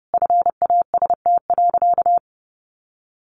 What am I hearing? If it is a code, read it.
Morse code, heard as a keyed tone, about 30 wpm, 700 Hz. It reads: FAST.
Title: Easy Phrases 30wpm